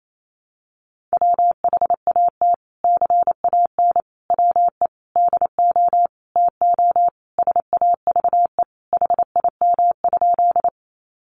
Morse code, heard as a keyed tone, about 28 wpm, 700 Hz. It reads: WHAT CAN WE DO TO SAVE HIM?
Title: Sentences from Top 1100 Words 28wpm